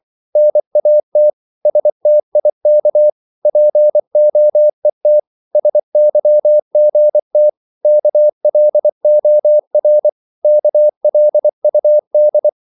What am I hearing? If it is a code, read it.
Morse code, heard as a keyed tone, about 24 wpm, 600 Hz. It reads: NAT STIK POET SYGT KLOR KLUD
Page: Korte danske ord 24wpm